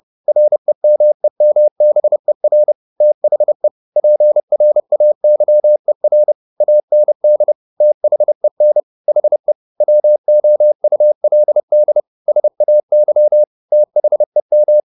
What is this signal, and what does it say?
Morse code, heard as a keyed tone, about 30 wpm, 600 Hz. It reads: REMEMBER THE PRAYER AND THEN HE WOULD SAY THEM
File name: the old man and the sea / the old man and the sea 1 30wpm